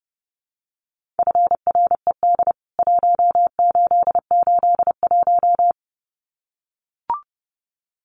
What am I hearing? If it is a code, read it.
Morse code, heard as a keyed tone, about 30 wpm, 700 Hz. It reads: FRED 1881
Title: CWT Exchange 30wpm